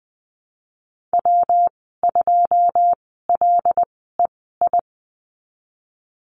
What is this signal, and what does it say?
Morse code, heard as a keyed tone, about 20 wpm, spaced as for 15 wpm, 700 Hz. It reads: W2LEI
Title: 128 Lesson 21g - Number 6 - Call Signs 20wpm